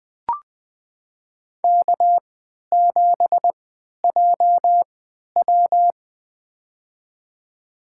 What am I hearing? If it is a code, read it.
Morse code, heard as a keyed tone, about 20 wpm, spaced as for 11 wpm, 700 Hz. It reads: K7JW
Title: USA Call Signs - 4 Characters - 3x Character Spacing - 20wpm